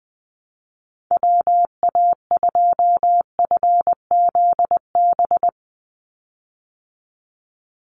Morse code, heard as a keyed tone, about 20 wpm, 700 Hz. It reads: WA2FZB
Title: Call Signs Repeated 2x - 20wpm